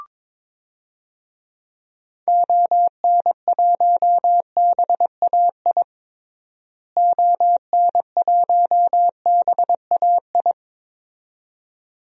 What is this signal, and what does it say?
Morse code, heard as a keyed tone, about 22 wpm, 700 Hz. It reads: ON1BAI ON1BAI
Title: DXCC-100 Call Signs - 6 Characters - Repeated 2x - 22wpm